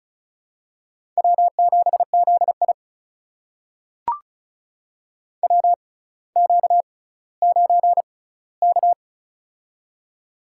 Morse code, heard as a keyed tone, about 36 wpm, spaced as for 18 wpm, 700 Hz. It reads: W7ZI WQ9K
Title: USA Call Signs - 4 Characters - 6x Character Spacing - 35wpm